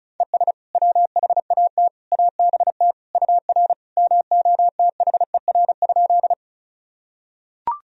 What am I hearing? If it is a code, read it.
Morse code, heard as a keyed tone, about 34 wpm, 700 Hz. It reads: ES WHAT ABT UR MOTHER?
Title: Sentences from Top 200 Words and Abbreviations 35wpm